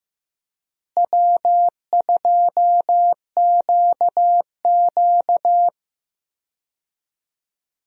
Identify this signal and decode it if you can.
Morse code, heard as a keyed tone, about 15 wpm, 700 Hz. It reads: W2QQ
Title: USA Call Signs - 4 Characters - Repeated 3x - 15wpm